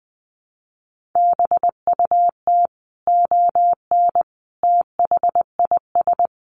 Morse code, heard as a keyed tone, about 20 wpm, 700 Hz. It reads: BUT ON THIS
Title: Sentences from Top 1400 Words 20wpm